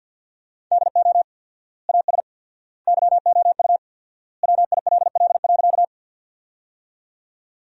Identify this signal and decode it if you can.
Morse code, heard as a keyed tone, about 53 wpm, 700 Hz. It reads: DO AS YOU WILL.
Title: 4x Word-Spacing - Sentences from Top 100 Words - 50wpm